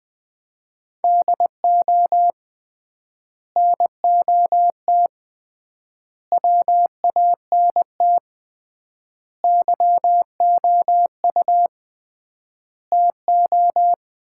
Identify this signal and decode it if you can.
Morse code, heard as a keyed tone, about 21 wpm, 700 Hz. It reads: DO NOT WANT YOU TO
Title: 3x Word-Spacing - Sets of 5 Words - 20wpm